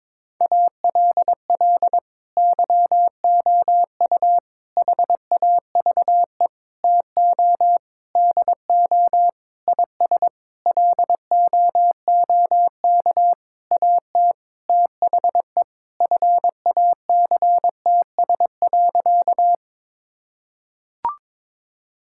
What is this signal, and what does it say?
Morse code, heard as a keyed tone, about 22 wpm, 700 Hz. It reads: ALL YOU HAVE TO DO IS LOOK AT THE FACTS.
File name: practice / 22wpm / Sentences with 8-Word Phrases - 22wpm